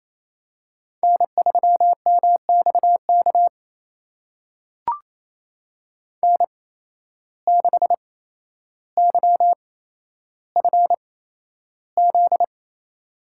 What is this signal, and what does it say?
Morse code, heard as a keyed tone, about 28 wpm, spaced as for 12 wpm, 700 Hz. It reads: N3MXK N6YFZ
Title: USA Call Signs - 5 Characters - 8x Character Spacing - 28wpm